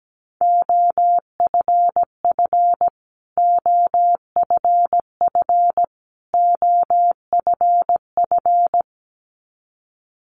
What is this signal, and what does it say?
Morse code, heard as a keyed tone, about 17 wpm, 700 Hz. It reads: OFF OFF OFF
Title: Top 500 Words Repeated 17wpm